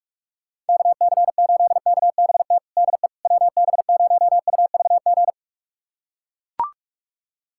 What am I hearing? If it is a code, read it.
Morse code, heard as a keyed tone, about 41 wpm, 700 Hz. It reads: KC8KBT DE WB0UVG
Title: USA Callsign DE Callsign - 6 Characters - 45wpm